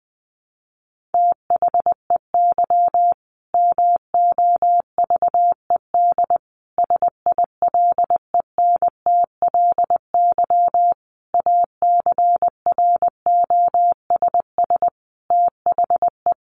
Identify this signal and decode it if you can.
Morse code, heard as a keyed tone, about 20 wpm, 700 Hz. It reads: THEY MOVED SILENTLY ACROSS THE
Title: A Sound Of Thunder 20wpm